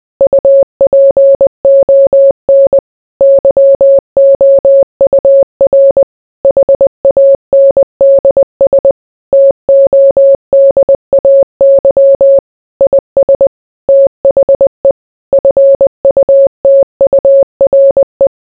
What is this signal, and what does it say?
Morse code, heard as a keyed tone, about 20 wpm, 550 Hz. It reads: UPON YOUR HANDS TODAY IS THE FUTURE